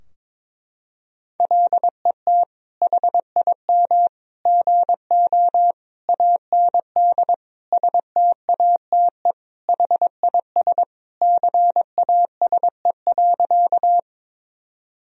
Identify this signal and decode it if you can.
Morse code, heard as a keyed tone, about 22 wpm, 700 Hz. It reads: LET HIM GO AND STATE HIS CASE.
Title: Sentences from Top 1000 Words - Review Part 2 - 22wpm